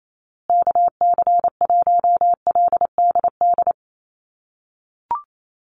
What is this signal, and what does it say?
Morse code, heard as a keyed tone, about 28 wpm, 700 Hz. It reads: KC1LDD
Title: USA Call Signs Repeated 2x 28wpm